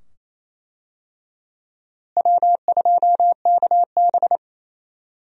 Morse code, heard as a keyed tone, about 27 wpm, 700 Hz. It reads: W2KB
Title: USA Call Signs - 4 Characters - Repeated 2x - 28wpm